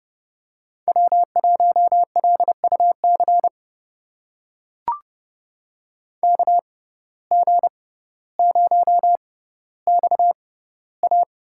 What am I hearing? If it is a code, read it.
Morse code, heard as a keyed tone, about 30 wpm, spaced as for 15 wpm, 700 Hz. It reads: W1LUC KG0XA
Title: USA Call Signs - 5 Characters - 6x Character Spacing - 30wpm